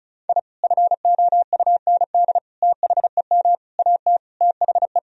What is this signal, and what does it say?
Morse code, heard as a keyed tone, about 34 wpm, 700 Hz. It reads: I FOUND THEM AT THE
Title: Sentences from Top 1000 Words - Review Part 2 - 35wpm